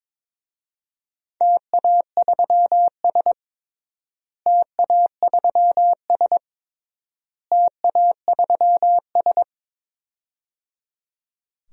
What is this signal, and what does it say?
Morse code, heard as a keyed tone, about 22 wpm, 700 Hz. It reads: TA3S TA3S TA3S
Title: DXCC-100 Call Signs - 4 Characters - Repeated 3x - 22wpm